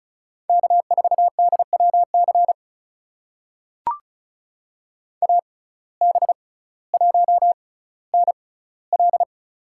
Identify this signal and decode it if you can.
Morse code, heard as a keyed tone, about 36 wpm, spaced as for 18 wpm, 700 Hz. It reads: K4DWC AB1NL
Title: USA Call Signs - 6x Character Spacing - 35wpm